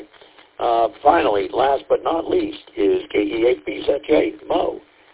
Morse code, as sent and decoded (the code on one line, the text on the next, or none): none